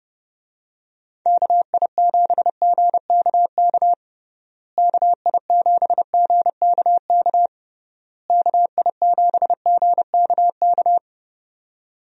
Morse code, heard as a keyed tone, about 30 wpm, 700 Hz. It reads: KI7GKK KI7GKK KI7GKK
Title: USA Call Signs - 6 Characters - Repeated 3x - 30wpm